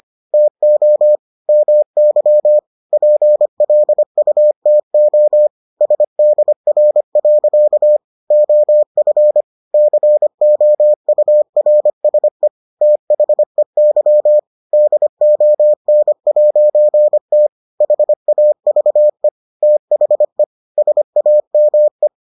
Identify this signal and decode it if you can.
Morse code, heard as a keyed tone, about 25 wpm, 600 Hz. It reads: TO MY PLUTO SDR. OF COURSE THEY DON'T HAVE THE SAME